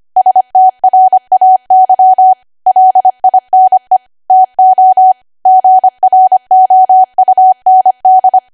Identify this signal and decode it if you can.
Morse code, heard as a keyed tone, about 25 wpm, 750 Hz. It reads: STRAY LINE TO GROUND
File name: cwa / W1AW-Level3 / 25WPM